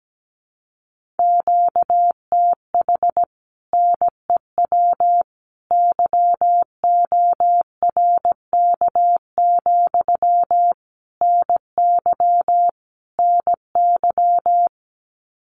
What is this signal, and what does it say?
Morse code, heard as a keyed tone, about 17 wpm, 700 Hz. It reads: QTH NEW YORK, NY NY
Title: Common QSO Phrases Encore 17wpm